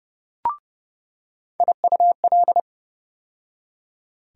Morse code, heard as a keyed tone, about 32 wpm, 700 Hz. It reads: IUL
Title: USA Call Sign Postfixes - 3 Letters 30wpm